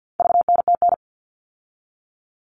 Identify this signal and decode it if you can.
Morse code, heard as a keyed tone, about 48 wpm, 700 Hz. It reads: VNTN
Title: All Cut Numbers - 4-Digit 50wpm